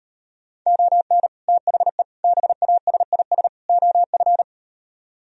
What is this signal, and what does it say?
Morse code, heard as a keyed tone, about 38 wpm, 700 Hz. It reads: ON THE BASIS OF